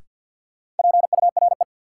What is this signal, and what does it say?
Morse code, heard as a keyed tone, about 53 wpm, 700 Hz. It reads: PURE
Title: Sentences from Top 1500 Words - New Word Drill Alt - 50wpm